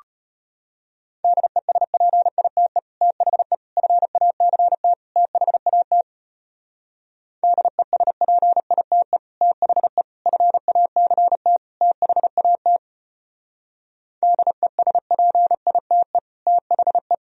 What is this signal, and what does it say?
Morse code, heard as a keyed tone, about 33 wpm, 700 Hz. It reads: DESPITE THE FACT THAT DESPITE THE FACT THAT DESPITE THE